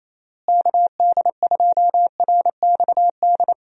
Morse code, heard as a keyed tone, about 28 wpm, 700 Hz. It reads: KD2RXD
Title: USA Call Signs - 6 Characters - Repeated 2x - 28wpm